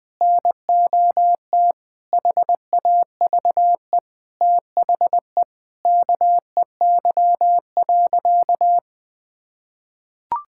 Morse code, heard as a keyed tone, about 20 wpm, 700 Hz. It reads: NOT HAVE THE KEY.
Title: Sentences with 5-Word Phrases - 20wpm